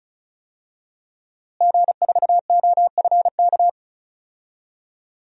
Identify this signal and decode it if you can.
Morse code, heard as a keyed tone, about 34 wpm, 700 Hz. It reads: G4OFK
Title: DXCC-100 Call Signs - 5 Characters - Repeated 3x - 35wpm